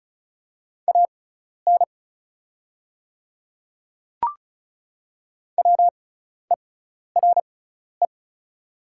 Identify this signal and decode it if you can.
Morse code, heard as a keyed tone, about 36 wpm, spaced as for 12 wpm, 700 Hz. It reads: AN WERE
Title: 6x Character Spacing - Top 100 Words - 35wpm